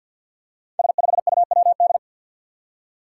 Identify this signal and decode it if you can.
Morse code, heard as a keyed tone, about 53 wpm, 700 Hz. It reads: S5UWD
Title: Five Challenging Letter-Number Combinations 50wpm